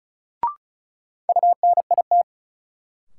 Morse code, heard as a keyed tone, about 36 wpm, 700 Hz. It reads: UNIT
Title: Sentences from Top 1000 Words - Review Part 2 - New Word Drill - 35wpm